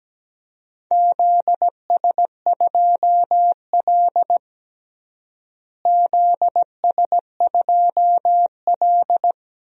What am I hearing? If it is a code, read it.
Morse code, heard as a keyed tone, about 17 wpm, 700 Hz. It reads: ZS2L ZS2L
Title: DXCC-100 Call Signs - 4 Characters - Repeated 2x - 17wpm